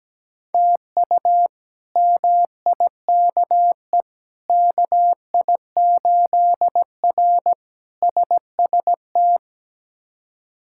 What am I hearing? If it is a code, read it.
Morse code, heard as a keyed tone, about 17 wpm, 700 Hz. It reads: TU MIKE KI8R SST
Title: K1USN SST QSO Practice - 17wpm